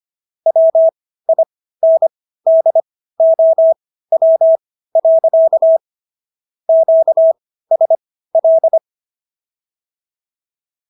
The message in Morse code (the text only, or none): WINDOW. QSL